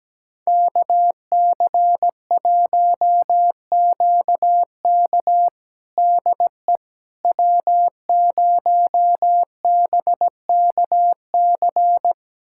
Morse code, heard as a keyed tone, about 17 wpm, 700 Hz. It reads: KC1QK DE W0BKC
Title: USA Callsign DE Callsign - 5 Characters - 17wpm